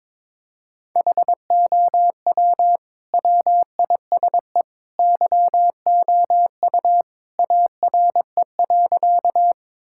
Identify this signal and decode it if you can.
Morse code, heard as a keyed tone, about 22 wpm, 700 Hz. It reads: HOW WISE YOU ARE.
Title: Sentences from Top 1600 Words 22wpm